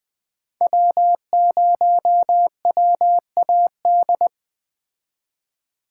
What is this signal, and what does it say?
Morse code, heard as a keyed tone, about 20 wpm, 700 Hz. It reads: W0WAD